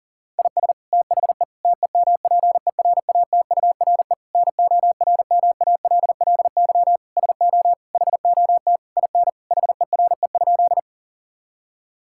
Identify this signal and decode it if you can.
Morse code, heard as a keyed tone, about 40 wpm, 700 Hz. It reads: IS THE TEMPERATURE NORMALLY SO HOT IN HERE?
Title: Sentences from Top 1000 Words - Review Part 6 - 40wpm